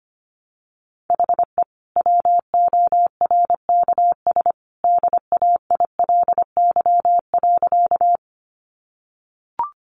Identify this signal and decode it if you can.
Morse code, heard as a keyed tone, about 25 wpm, 700 Hz. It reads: HE WORKS DAILY.
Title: Sentences from Top 1100 Words 25wpm